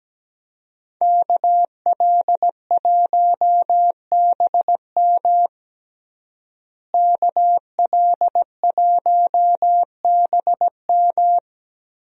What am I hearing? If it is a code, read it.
Morse code, heard as a keyed tone, about 17 wpm, 700 Hz. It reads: KL1BM KL1BM
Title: USA Call Signs - 5 Characters - Repeated 3x - 17wpm